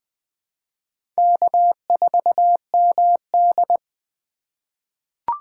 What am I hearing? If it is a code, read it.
Morse code, heard as a keyed tone, about 20 wpm, 700 Hz. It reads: K4MD
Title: USA Call Signs - 4 Characters - 3x Character Spacing - 20wpm